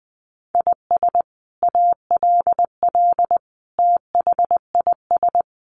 Morse code, heard as a keyed tone, about 20 wpm, 700 Hz. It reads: IS ALL THIS
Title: Sentences from Top 1400 Words 20wpm